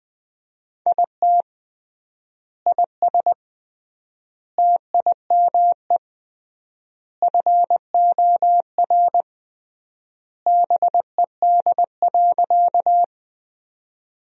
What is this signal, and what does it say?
Morse code, heard as a keyed tone, about 20 wpm, 700 Hz. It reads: IT IS TIME FOR BED.
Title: 3x Word-Spacing - Sentences from Top 500 Words - 20wpm